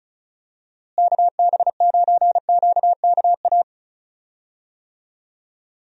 Morse code, heard as a keyed tone, about 36 wpm, 700 Hz. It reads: KB9QKA
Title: USA Call Signs - 6 Characters - Repeated 2x - 35wpm